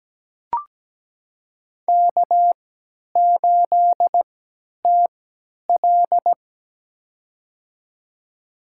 Morse code, heard as a keyed tone, about 17 wpm, spaced as for 10 wpm, 700 Hz. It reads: K8TL
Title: USA Call Signs - 4 Characters - 3x Character Spacing - 17wpm